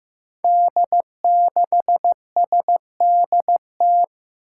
Morse code, heard as a keyed tone, about 15 wpm, 700 Hz. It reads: D6SDT
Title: Five Challenging Letter-Number Combinations 15wpm